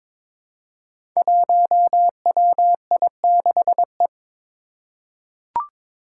Morse code, heard as a keyed tone, about 22 wpm, 700 Hz. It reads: 1WI6E
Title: Five Challenging Letter-Number Combinations 22wpm